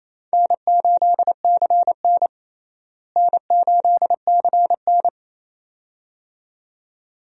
Morse code, heard as a keyed tone, about 28 wpm, 700 Hz. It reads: N8CN N8CN
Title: USA Call Signs - 4 Characters - Repeated 2x - 28wpm